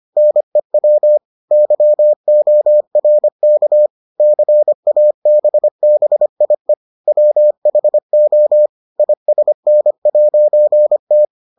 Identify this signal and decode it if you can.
Morse code, heard as a keyed tone, about 25 wpm, 600 Hz. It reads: NEW YORK CABBIE WHO ISN'T